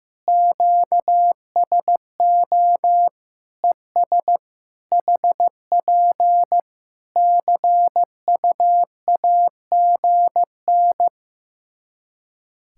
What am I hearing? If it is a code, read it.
Morse code, heard as a keyed tone, about 15 wpm, 700 Hz. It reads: QSO ES HP CUAGN
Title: Practice QSO - Exchange 4 - Ending - 15wpm